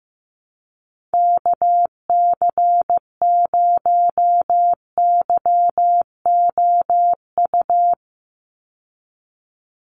Morse code, heard as a keyed tone, about 15 wpm, 700 Hz. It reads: KC0YOU
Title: USA Call Signs 15wpm